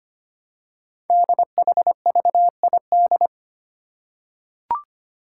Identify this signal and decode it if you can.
Morse code, heard as a keyed tone, about 25 wpm, 700 Hz. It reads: DHVID
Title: Code Groups - 5 Letters - 25wpm